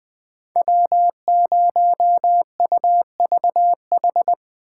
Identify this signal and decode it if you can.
Morse code, heard as a keyed tone, about 24 wpm, 700 Hz. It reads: W0UVH
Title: Speed Racing - USA 1x3 Call Signs 20wpm